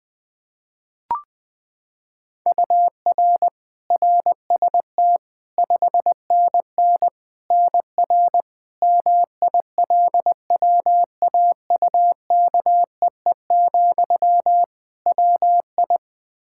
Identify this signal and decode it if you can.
Morse code, heard as a keyed tone, about 20 wpm, 700 Hz. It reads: UR RST 5NN NR MILWAUKEE, WI
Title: Practice QSO - Exchange 1 - Beginning - 20wpm